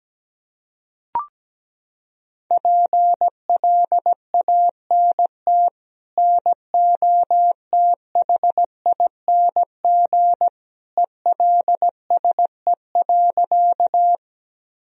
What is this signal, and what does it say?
Morse code, heard as a keyed tone, about 17 wpm, 700 Hz. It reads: PLANT NOTHING ELSE.
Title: Sentences from Top 1000 Words - Review Part 1 - 17wpm